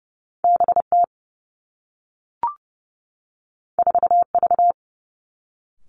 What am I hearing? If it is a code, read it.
Morse code, heard as a keyed tone, about 30 wpm, 700 Hz. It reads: B T 4 V
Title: Two Challenging Letter-Number Combinations 30wpm